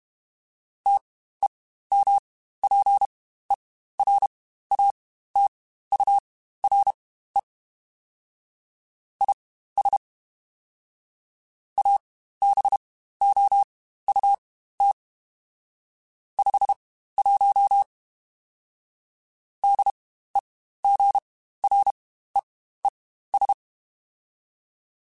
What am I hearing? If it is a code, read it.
Morse code, heard as a keyed tone, about 32 wpm, spaced as for 15 wpm, 800 Hz. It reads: TEMPERATURE IS ABOUT 51 DEGREES
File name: Spaced Characters / Spaced Characters-mp3s / Weather 4 21 sp @30